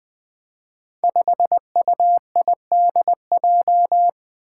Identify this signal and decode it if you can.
Morse code, heard as a keyed tone, about 20 wpm, 700 Hz. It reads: 5UIDJ
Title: Five Challenging Letter-Number Combinations 20wpm